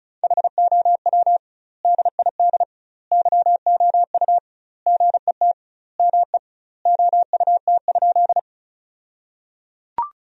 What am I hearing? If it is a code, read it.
Morse code, heard as a keyed tone, about 34 wpm, 700 Hz. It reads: HOW DID YOU GET ME OUT?
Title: 2x Word-Spacing - Sentences from Top 200 Words - 35wpm